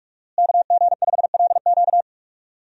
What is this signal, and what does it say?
Morse code, heard as a keyed tone, about 46 wpm, 700 Hz. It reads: KG5LQ
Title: USA Call Signs - 5 Characters - Repeated 3x - 45wpm